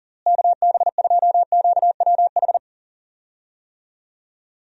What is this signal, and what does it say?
Morse code, heard as a keyed tone, about 40 wpm, 700 Hz. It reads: KB2QWH
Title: USA Call Signs - 6 Characters - Repeated 3x - 40wpm